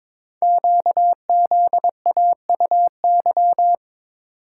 Morse code, heard as a keyed tone, about 22 wpm, 700 Hz. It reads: QZAUY
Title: Code Groups - 5 Letters - 22wpm